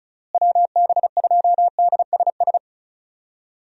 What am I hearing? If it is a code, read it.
Morse code, heard as a keyed tone, about 36 wpm, 700 Hz. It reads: WB2DSS
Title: USA Call Signs - 6 Characters - Repeated 2x - 35wpm